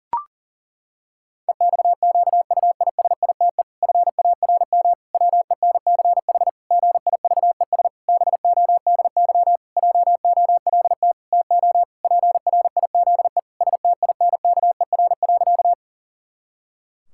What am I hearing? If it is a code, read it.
Morse code, heard as a keyed tone, about 40 wpm, 700 Hz. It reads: EXQUISITE FARM WENCH GIVES BODY JOLT TO PRIZE STINKER.
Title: Pangram Challenge v2 - 40wpm